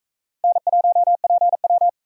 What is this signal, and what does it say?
Morse code, heard as a keyed tone, about 44 wpm, 700 Hz. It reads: N1PW